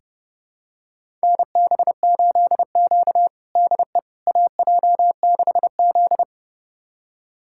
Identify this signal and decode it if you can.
Morse code, heard as a keyed tone, about 30 wpm, 700 Hz. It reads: NB8Q DE AJ6Z
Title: USA Callsign DE Callsign - 4 Characters - 30wpm